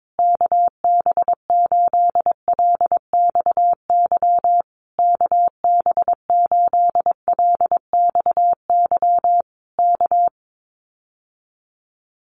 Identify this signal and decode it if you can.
Morse code, heard as a keyed tone, about 22 wpm, 700 Hz. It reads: KB8LXY KB8LXY K
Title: Common QSO Phrases Encore 22wpm